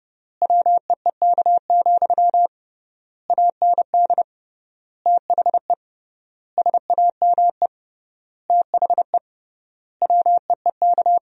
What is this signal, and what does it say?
Morse code, heard as a keyed tone, about 32 wpm, 700 Hz. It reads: WEEK, AND THE SAME THE WEEK
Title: 3x Word-Spacing - Sentences from Top 500 Words - 30wpm